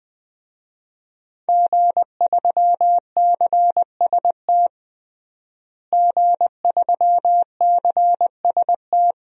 Morse code, heard as a keyed tone, about 20 wpm, 700 Hz. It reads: G3CST G3CST
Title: DXCC-100 Call Signs - 5 Characters - Repeated 3x - 20wpm